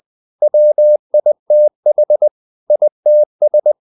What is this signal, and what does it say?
Morse code, heard as a keyed tone, about 20 wpm, 600 Hz. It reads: WITH ITS
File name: the old man and the sea / the old man and the sea part 1 20wpm